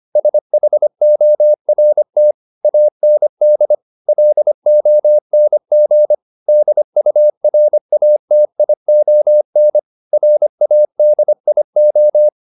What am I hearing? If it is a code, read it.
Morse code, heard as a keyed tone, about 25 wpm, 600 Hz. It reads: SHORT AND LONG DURATION RADIO